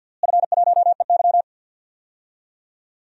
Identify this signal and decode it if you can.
Morse code, heard as a keyed tone, about 47 wpm, 700 Hz. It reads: F1EY
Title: DXCC-100 Call Signs - 4 Characters - Repeated 3x - 50wpm